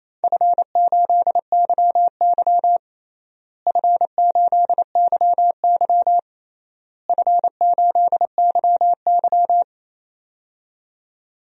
Morse code, heard as a keyed tone, about 27 wpm, 700 Hz. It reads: F8YY F8YY F8YY
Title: DXCC-100 Call Signs - 4 Characters - Repeated 3x - 28wpm